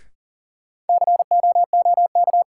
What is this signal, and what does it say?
Morse code, heard as a keyed tone, about 40 wpm, 700 Hz. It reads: COOK
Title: Sentences from Top 700 Words - New Word Drill - 40wpm